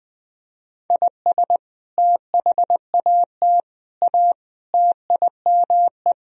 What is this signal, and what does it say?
Morse code, heard as a keyed tone, about 20 wpm, 700 Hz. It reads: IS THAT A TIME
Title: Sentences from Top 1000 Words - Review Part 2 - 20wpm